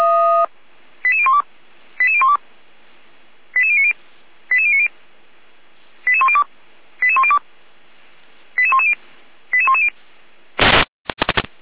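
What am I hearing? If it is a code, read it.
Five-tone selective call (ZVEI), heard as quick runs of five tones, tones of about 80 ms. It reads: 88012 88012 88090 88090 88102 88102 88110 88110
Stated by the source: Received on SDR; ZVEI five-tone call 88199, channel 455, recorded by Feuerwehr Orlamünde.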